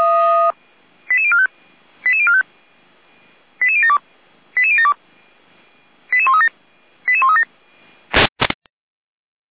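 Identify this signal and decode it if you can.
Five-tone selective call (ZVEI), heard as quick runs of five tones, tones of about 70 ms. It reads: 88045 88045 88072 88072 88137 88137
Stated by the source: ZVEI five-tone selective call received with an SDR.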